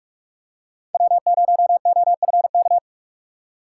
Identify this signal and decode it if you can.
Morse code, heard as a keyed tone, about 47 wpm, 700 Hz. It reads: W0OFK
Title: USA Call Signs - 5 Characters - 6x Character Spacing - 45wpm